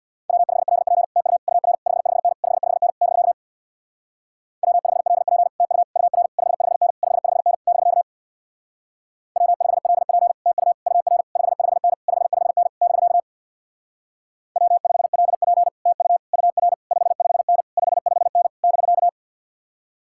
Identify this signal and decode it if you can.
Morse code, heard as a keyed tone, about 67 wpm, 700 Hz. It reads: W5LP TU UR 55N 55N <BK> W5LP TU UR 55N 55N <BK> W5LP TU UR 55N 55N <BK> W5LP TU UR 55N 55N <BK>